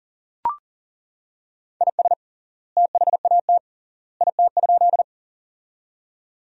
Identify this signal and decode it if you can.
Morse code, heard as a keyed tone, about 39 wpm, 700 Hz. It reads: IS THAT IT?
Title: 3x Word-Spacing - Sentences from Top 100 Words - 40wpm